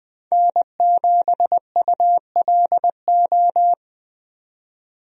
Morse code, heard as a keyed tone, about 20 wpm, 700 Hz. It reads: N7ULO
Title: USA Call Signs - 5 Characters - 20wpm